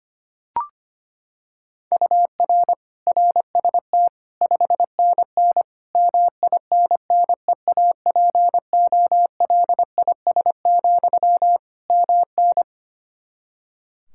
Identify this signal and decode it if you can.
Morse code, heard as a keyed tone, about 25 wpm, 700 Hz. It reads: UR RST 5NN MINNEAPOLIS, MN
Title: Practice QSO - Exchange 1 - Beginning - 25wpm